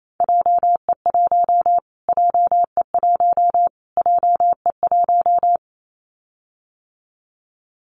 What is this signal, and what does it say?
Morse code, heard as a keyed tone, about 28 wpm, 700 Hz. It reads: JE1 JE1 JE1
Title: Callsign Prefixes for Top 25 DXCC - Repeated 3x - 28wpm